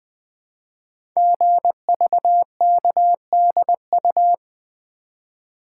Morse code, heard as a keyed tone, about 20 wpm, 700 Hz. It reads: GVKDU
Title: Code Groups - 5 Letter:Numbers - 20wpm